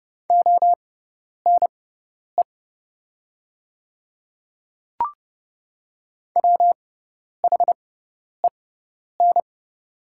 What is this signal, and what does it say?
Morse code, heard as a keyed tone, about 31 wpm, spaced as for 10 wpm, 700 Hz. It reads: ONE WHEN
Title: 6x Character Spacing - Top 100 Words - 30wpm